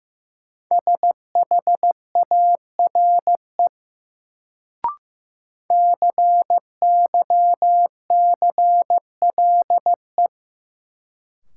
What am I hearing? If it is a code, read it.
Morse code, heard as a keyed tone, about 15 wpm, 700 Hz. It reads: SHARE CYCLE
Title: Sentences from Top 900 Words - New Word Drill - 15wpm